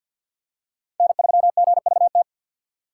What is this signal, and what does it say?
Morse code, heard as a keyed tone, about 52 wpm, 700 Hz. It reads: N3GVT
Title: USA Call Signs - 5 Characters - 2x Character Spacing - 50wpm